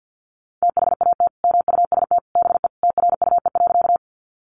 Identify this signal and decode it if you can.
Morse code, heard as a keyed tone, about 50 wpm, 700 Hz. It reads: THAT MUST BE TRUE.
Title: Sentences from Top 400 Words 50wpm